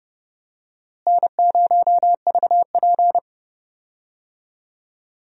Morse code, heard as a keyed tone, about 30 wpm, 700 Hz. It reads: N0VP
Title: USA Call Signs - 4 Characters - Repeated 2x - 30wpm